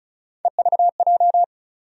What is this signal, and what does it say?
Morse code, heard as a keyed tone, about 34 wpm, 700 Hz. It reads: EVJ
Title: USA Call Sign Postfixes - 3 Letters 35wpm